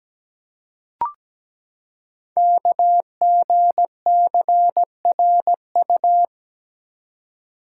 Morse code, heard as a keyed tone, about 17 wpm, 700 Hz. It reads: KGCRU
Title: Code Groups - 5 Letters - 17wpm